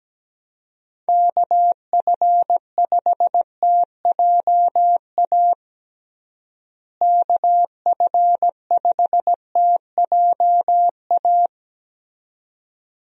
Morse code, heard as a keyed tone, about 17 wpm, 700 Hz. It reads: KF5TJA KF5TJA
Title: USA Call Signs - 6 Characters - Repeated 2x - 17wpm